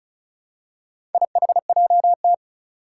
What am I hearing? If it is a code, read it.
Morse code, heard as a keyed tone, about 36 wpm, 700 Hz. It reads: IHJT
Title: Four Challenging Letter-Number Combinations 35wpm